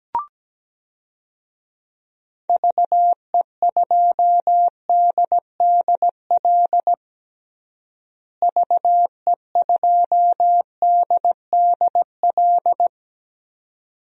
Morse code, heard as a keyed tone, about 17 wpm, 700 Hz. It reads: VE2DDL VE2DDL
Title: DXCC-100 Call Signs - 6 Characters - Repeated 2x - 17wpm